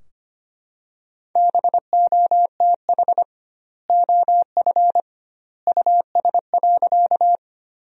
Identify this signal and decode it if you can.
Morse code, heard as a keyed tone, about 25 wpm, 700 Hz. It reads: BOTH OF US.
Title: 2x Word-Spacing - Sentences from Top 300 Words - 25wpm